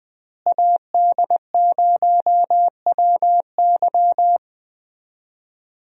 Morse code, heard as a keyed tone, about 20 wpm, 700 Hz. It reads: AD0WY
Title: USA Call Signs - 5 Characters - Repeated 3x - 20wpm